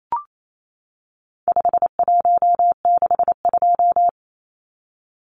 Morse code, heard as a keyed tone, about 28 wpm, 700 Hz. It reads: 5162
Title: Numbers - 4-Digit 28wpm